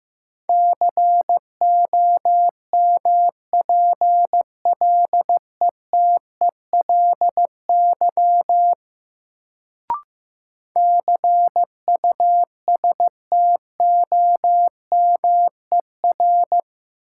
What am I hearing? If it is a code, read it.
Morse code, heard as a keyed tone, about 15 wpm, 700 Hz. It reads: COMPLETELY CUSTOMER
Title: Sentences from Top 1100 Words - New Word Drill Alt - 15wpm